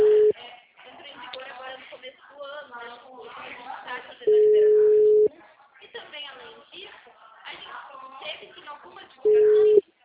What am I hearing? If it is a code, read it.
Ringback tone (Europe, 425 Hz), with speech.